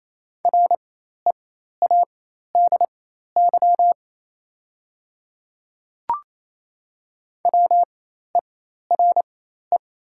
Morse code, heard as a keyed tone, about 28 wpm, spaced as for 13 wpm, 700 Hz. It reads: READY WERE
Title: 4x Character Spacing - Top 500 Words - 28wpm